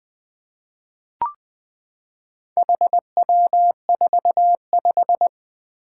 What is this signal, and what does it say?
Morse code, heard as a keyed tone, about 20 wpm, 700 Hz. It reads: HW45
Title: Four Challenging Letter-Number Combinations 20wpm